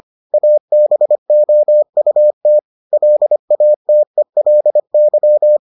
Morse code, heard as a keyed tone, about 25 wpm, 600 Hz. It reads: ABOUT LATELY